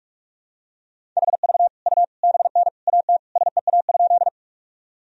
Morse code, heard as a keyed tone, about 46 wpm, 700 Hz. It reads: HV U BN AT SEA?
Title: Sentences from Top 300 Words and Abbreviations 45wpm